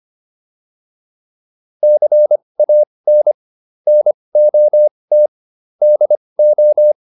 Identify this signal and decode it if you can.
Morse code, heard as a keyed tone, about 25 wpm, spaced as for 20 wpm, 600 Hz. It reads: CAN NOT DO